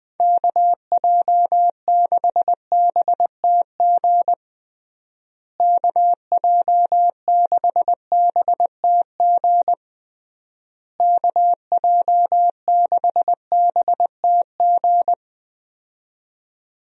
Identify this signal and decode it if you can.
Morse code, heard as a keyed tone, about 20 wpm, 700 Hz. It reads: KJ6BTG KJ6BTG KJ6BTG
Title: USA Call Signs - 6 Characters - Repeated 3x - 20wpm